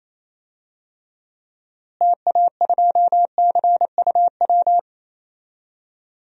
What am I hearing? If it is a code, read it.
Morse code, heard as a keyed tone, about 28 wpm, 700 Hz. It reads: TA2CUW
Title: DXCC-100 Call Signs - 6 Characters - Repeated 2x - 28wpm